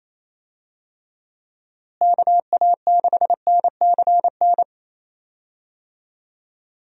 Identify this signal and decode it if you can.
Morse code, heard as a keyed tone, about 28 wpm, 700 Hz. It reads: KA6NCN
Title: USA Call Signs - 6 Characters - Repeated 2x - 28wpm